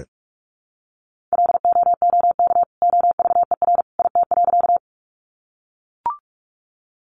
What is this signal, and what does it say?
Morse code, heard as a keyed tone, about 44 wpm, 700 Hz. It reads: LOOK OVER IT.
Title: Sentences from Top 100 Words 45wpm